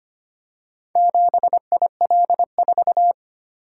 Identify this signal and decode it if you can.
Morse code, heard as a keyed tone, about 25 wpm, 700 Hz. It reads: 7IL4
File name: practice / 25wpm / Four Letter-Number Combinations - 25wpm